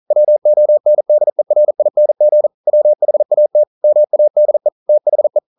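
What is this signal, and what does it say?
Morse code, heard as a keyed tone, about 42 wpm, 600 Hz. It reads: WONDERING WHAT MADE THE
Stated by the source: Morse code practice at 41 wpm, from CW med Gnister.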